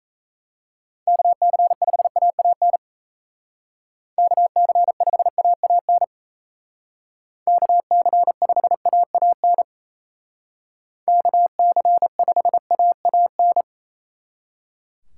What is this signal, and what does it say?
Morse code, heard as a keyed tone, about 35 wpm, 700 Hz. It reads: KC5AAN KC5AAN KC5AAN KC5AAN
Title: Speed Racing - USA 2x3 Call Signs 28wpm